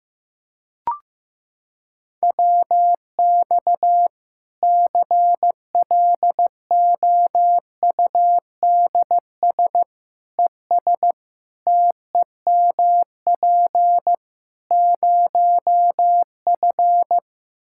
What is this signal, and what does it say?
Morse code, heard as a keyed tone, about 15 wpm, 700 Hz. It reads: WX CLOUDS ES TEMP 0F
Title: Practice QSO - Exchange 2 - Info - 15wpm